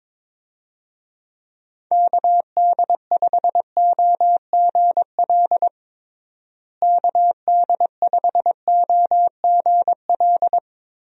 Morse code, heard as a keyed tone, about 22 wpm, 700 Hz. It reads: KD5OGL KD5OGL
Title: USA Call Signs - 6 Characters - Repeated 3x - 22wpm